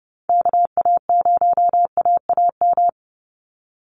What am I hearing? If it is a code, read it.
Morse code, heard as a keyed tone, about 30 wpm, 700 Hz. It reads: KA0AAM
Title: Call Signs Repeated 3x - 30wpm